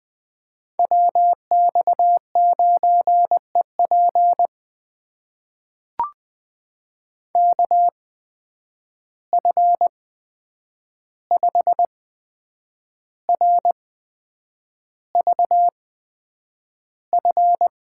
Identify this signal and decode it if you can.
Morse code, heard as a keyed tone, about 20 wpm, spaced as for 8 wpm, 700 Hz. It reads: WX9EP KF5RVF
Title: USA Call Signs - 8x Character Spacing - 20wpm